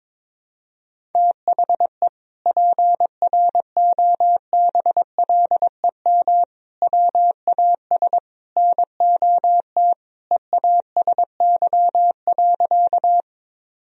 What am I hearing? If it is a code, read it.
Morse code, heard as a keyed tone, about 22 wpm, 700 Hz. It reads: THE PROBLEM WAS NOT EASY.
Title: Sentences from Top 1000 Words - Review Part 2 - 22wpm